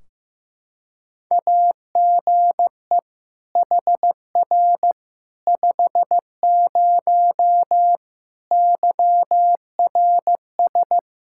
Morse code, heard as a keyed tone, about 15 wpm, 700 Hz. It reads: AGE HR 50 YRS
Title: K1USN SST QSO Practice 15wpm